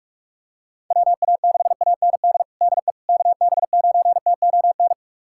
Morse code, heard as a keyed tone, about 46 wpm, 700 Hz. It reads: WA6AND DE KB9TON
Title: USA Callsign DE Callsign 45wpm